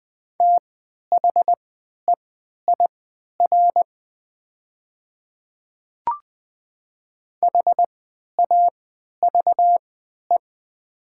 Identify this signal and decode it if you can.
Morse code, heard as a keyed tone, about 20 wpm, spaced as for 11 wpm, 700 Hz. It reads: THEIR HAVE
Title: 3x Character Spacing - Top 100 Words - 20wpm